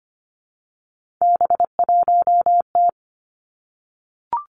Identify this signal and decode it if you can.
Morse code, heard as a keyed tone, about 25 wpm, 700 Hz. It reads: B1T
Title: Three Challenging Letter-Number Combinations 25wpm